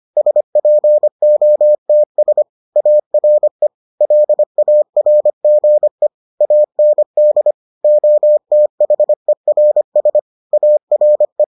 Morse code, heard as a keyed tone, about 25 wpm, 600 Hz. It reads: SPOTS ARE LARGE AND OTHERS ARE